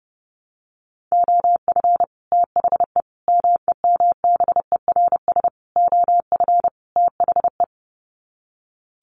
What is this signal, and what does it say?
Morse code, heard as a keyed tone, about 30 wpm, 700 Hz. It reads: OF THE MEMBERS OF THE
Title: Sets of 5 Words Encore2 30wpm